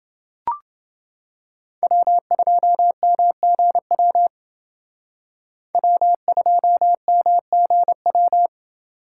Morse code, heard as a keyed tone, about 28 wpm, 700 Hz. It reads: W2MGW W2MGW